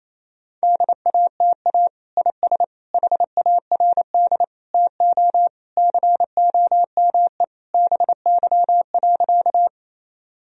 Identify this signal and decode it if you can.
Morse code, heard as a keyed tone, about 28 wpm, 700 Hz. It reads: DATA IS HARD TO COME BY.
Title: Sentences from Top 1000 Words - Review Part 3 - 28wpm